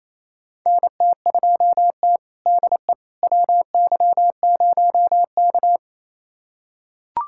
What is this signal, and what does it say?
Morse code, heard as a keyed tone, about 28 wpm, 700 Hz. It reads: NT2T DE WY0K
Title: USA Callsign DE Callsign - 4 Characters - 28wpm